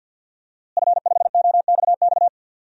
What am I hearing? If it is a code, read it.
Morse code, heard as a keyed tone, about 50 wpm, 700 Hz. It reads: F5OXX
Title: DXCC-100 Call Signs - 5 Characters - Repeated 2x - 50wpm